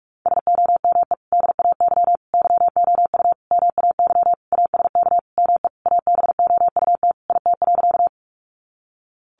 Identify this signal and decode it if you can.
Morse code, heard as a keyed tone, about 44 wpm, 700 Hz. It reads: SOME DAY YOU MAY ASK ME ABOUT IT.
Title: Sentences from Top 200 Words 45wpm